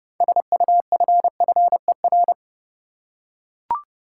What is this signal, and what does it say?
Morse code, heard as a keyed tone, about 30 wpm, 700 Hz. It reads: SUFFER
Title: Six Letter Words - 30wpm